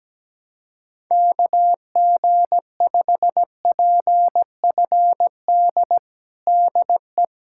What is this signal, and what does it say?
Morse code, heard as a keyed tone, about 17 wpm, 700 Hz. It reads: KG5PFD DE
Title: USA Callsign DE Callsign 17wpm